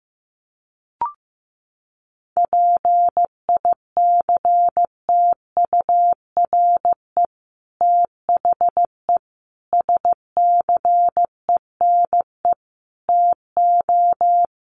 Morse code, heard as a keyed tone, about 15 wpm, 700 Hz. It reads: PICTURE THE SCENE TO
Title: Sentences from Top 1100 Words 15wpm